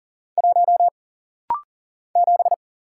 Morse code, heard as a keyed tone, about 40 wpm, 700 Hz. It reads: 1 7
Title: Single Letter-Number - Rapid-Fire 40wpm